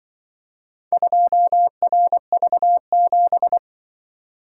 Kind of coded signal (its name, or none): Morse code